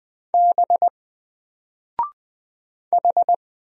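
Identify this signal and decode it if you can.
Morse code, heard as a keyed tone, about 20 wpm, 700 Hz. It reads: B H
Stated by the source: Single letter and number Morse drill at 20 wpm, from N4LCD.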